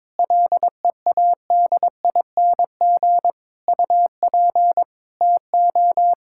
Morse code, heard as a keyed tone, about 22 wpm, 700 Hz. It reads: LEADING UP TO
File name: Sentences with 7-Word Phrases - 22wpm